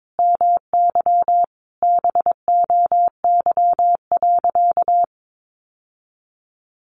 Morse code, heard as a keyed tone, about 22 wpm, 700 Hz. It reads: MY BOY.
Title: Sentences from Top 1000 Words 22wpm